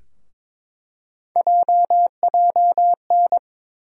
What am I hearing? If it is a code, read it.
Morse code, heard as a keyed tone, about 22 wpm, 700 Hz. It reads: JJN
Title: USA Call Sign Postfixes - 3 Letters 22wpm